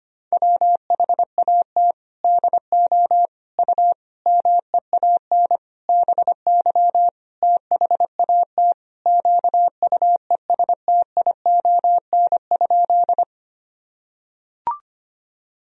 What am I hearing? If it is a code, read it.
Morse code, heard as a keyed tone, about 25 wpm, 700 Hz. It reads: WHAT DO U MEAN BY THAT QUESTION?
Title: Sentences from Top 400 Words and Abbreviations 25wpm